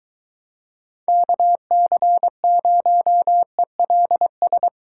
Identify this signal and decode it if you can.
Morse code, heard as a keyed tone, about 22 wpm, 700 Hz. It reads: KC0ELS